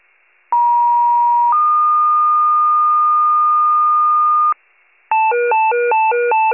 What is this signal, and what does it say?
Two-tone page, heard as a one-second tone, then a three-second tone, about 950 then 1220 Hz, with noise. It is heard from 0.5 to 4.5 s.